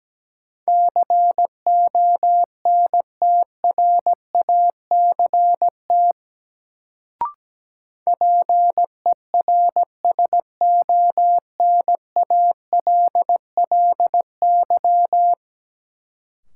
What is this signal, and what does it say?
Morse code, heard as a keyed tone, about 17 wpm, 700 Hz. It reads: CONTRACT PERSONALLY
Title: Sentences from Top 1400 Words - New Word Drill Alt - 17wpm